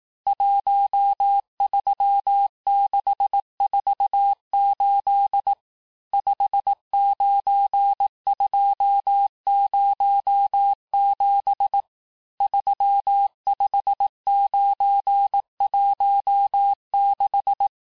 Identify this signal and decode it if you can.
Morse code, heard as a keyed tone, about 18 wpm, 800 Hz. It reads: 13648 59207 35916
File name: cours-du-lundi files / cours-1594--teste-chiffre-vitesse-17mpm